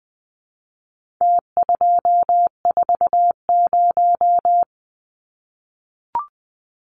Morse code, heard as a keyed tone, about 20 wpm, 700 Hz. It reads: T240
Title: Numbers - Contest Style - 4-Digit 20wpm